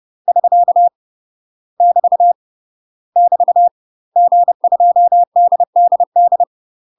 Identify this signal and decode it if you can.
Morse code, heard as a keyed tone, about 30 wpm, 700 Hz. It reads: <SK> <BT> <BT> G2DDD
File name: UK MASTER QSOs TRAINING / 30WPM / UK MASTER QSOs 0000